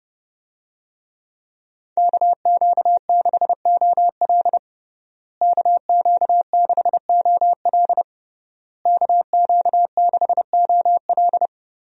Morse code, heard as a keyed tone, about 30 wpm, 700 Hz. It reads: KQ6OL KQ6OL KQ6OL
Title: USA Call Signs - 5 Characters - Repeated 3x - 30wpm